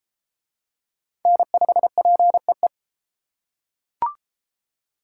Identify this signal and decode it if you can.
Morse code, heard as a keyed tone, about 30 wpm, 700 Hz. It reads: N5PEE